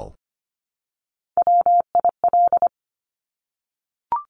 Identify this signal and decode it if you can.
Morse code, heard as a keyed tone, about 25 wpm, 700 Hz. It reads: WIL
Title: Top Three Letter Combinations in Words 25wpm